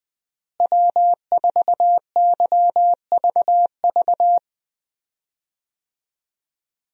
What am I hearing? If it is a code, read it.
Morse code, heard as a keyed tone, about 21 wpm, 700 Hz. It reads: W4YVV
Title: USA Call Signs - 5 Characters - Repeated 2x - 20wpm